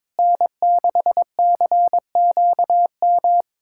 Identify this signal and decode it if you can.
Morse code, heard as a keyed tone, about 22 wpm, 700 Hz. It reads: N6CQM
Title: USA Callsign DE Callsign - 5 Characters - 22wpm